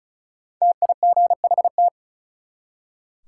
Morse code, heard as a keyed tone, about 36 wpm, 700 Hz. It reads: TIGHT
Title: Five Letter Words - 35wpm